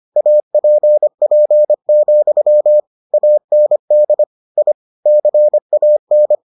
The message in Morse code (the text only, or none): APP, AND I CAN